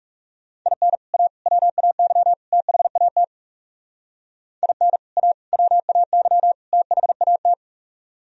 Morse code, heard as a keyed tone, about 40 wpm, 700 Hz. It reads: IN A WAY THAT IN A WAY THAT